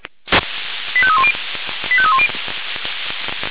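Five-tone selective call (ZVEI), heard as quick runs of five tones, tones of about 70 ms. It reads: 85310 85310
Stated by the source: Five-tone ZVEI selective call received by SDR, from the server of Feuerwehr Orlamünde (fire brigade).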